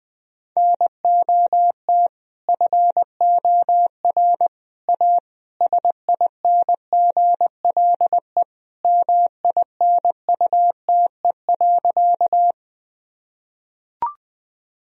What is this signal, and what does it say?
Morse code, heard as a keyed tone, about 20 wpm, 700 Hz. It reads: NOT FOR A SINGLE MINUTE.
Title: Sentences from Top 1000 Words - Review Part 4 - 20wpm